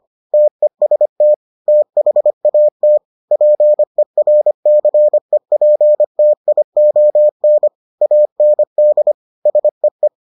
Morse code, heard as a keyed tone, about 25 wpm, 600 Hz. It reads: TEST THAT PERCEPTION AND SEE